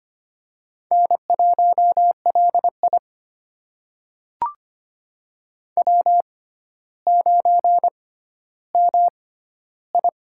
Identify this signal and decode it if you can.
Morse code, heard as a keyed tone, about 25 wpm, spaced as for 13 wpm, 700 Hz. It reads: N1LI W9MI
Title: USA Call Signs - 4 Characters - 6x Character Spacing - 25wpm